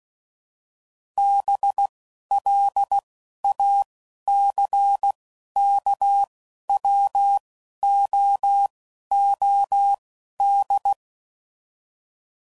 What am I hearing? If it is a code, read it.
Morse code, heard as a keyed tone, about 16 wpm, spaced as for 12 wpm, 800 Hz. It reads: BLACKWOOD
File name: CW Code Groups @10WPM / fromSM's / Cities